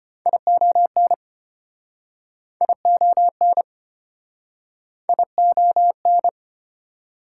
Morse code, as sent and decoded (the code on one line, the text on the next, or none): .. --- -. / .. --- -. / .. --- -.
ION ION ION